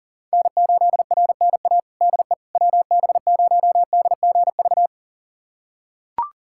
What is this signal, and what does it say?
Morse code, heard as a keyed tone, about 40 wpm, 700 Hz. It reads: N8RNA DE WB0DGV
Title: USA Callsign DE Callsign 40wpm